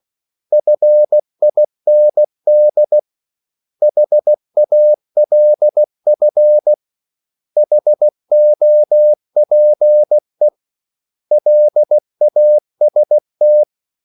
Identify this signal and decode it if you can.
Morse code, heard as a keyed tone, about 16 wpm, 600 Hz. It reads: FIND HALF HOPE LAST
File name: CW / Practice / 500MostCommonEnglishWords16